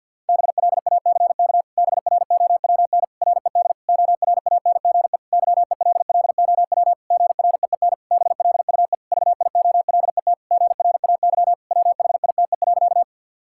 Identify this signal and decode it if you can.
Morse code, heard as a keyed tone, about 51 wpm, 700 Hz. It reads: BLACK BROWN RED ORANGE YELLOW GREEN BLUE VIOLET GRAY WHITE.
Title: Pangram Challenge v2 - 50wpm